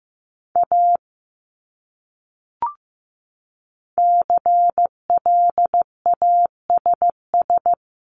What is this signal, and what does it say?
Morse code, heard as a keyed tone, about 15 wpm, 700 Hz. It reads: A CLASS
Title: Top 500 Words Repeated 15wpm